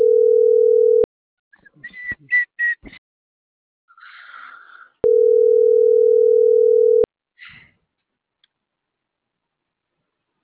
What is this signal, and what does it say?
Ringback tone (North America), 440 + 480 Hz.